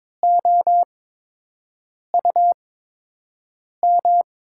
Morse code, heard as a keyed tone, about 22 wpm, spaced as for 6 wpm, 700 Hz. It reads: OUM
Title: USA Call Signs - 8x Character Spacing - 22wpm